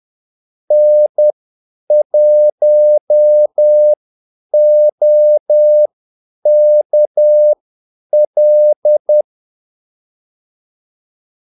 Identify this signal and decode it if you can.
Morse code, heard as a keyed tone, about 10 wpm, spaced as for 8 wpm, 600 Hz. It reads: N1OKL
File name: Callsigns-10-8wpm